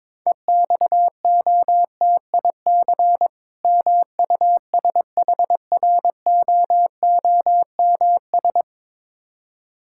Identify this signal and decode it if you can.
Morse code, heard as a keyed tone, about 21 wpm, 690 Hz. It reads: EXOTIC MUSHROOMS
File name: Difficult Words with Hints - 22wpm